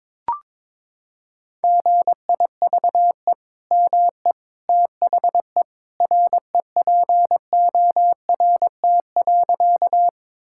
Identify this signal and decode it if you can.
Morse code, heard as a keyed tone, about 22 wpm, 700 Hz. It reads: GIVE ME THE REPORT.
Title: Sentences from Top 1000 Words - Review Part 6 - 22wpm